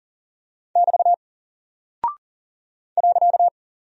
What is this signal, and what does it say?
Morse code, heard as a keyed tone, about 38 wpm, 700 Hz. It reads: <BT> .
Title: Single Character (Letter:Number:Punctuation:Prosign) Alt 40wpm